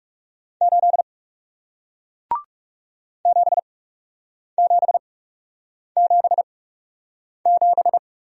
Morse code, heard as a keyed tone, about 41 wpm, spaced as for 8 wpm, 700 Hz. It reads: 8 7777